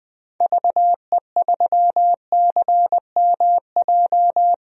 Morse code, heard as a keyed tone, about 20 wpm, 700 Hz. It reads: VE3CMJ
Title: DXCC-100 Call Signs - 6 Characters - 20wpm